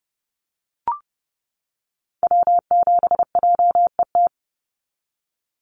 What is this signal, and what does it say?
Morse code, heard as a keyed tone, about 30 wpm, 700 Hz. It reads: W7JET
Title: NA SOTA Activator Callsigns 30wpm